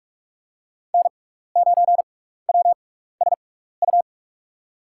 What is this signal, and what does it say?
Morse code, heard as a keyed tone, about 48 wpm, 700 Hz. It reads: N 9 W S U
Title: USA Call Signs - 5 Characters - 6x Character Spacing - 45wpm